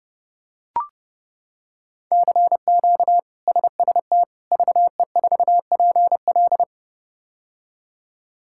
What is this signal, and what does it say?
Morse code, heard as a keyed tone, about 30 wpm, 700 Hz. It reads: CQ SST VE4PL
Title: K1USN SST QSO Practice - 30wpm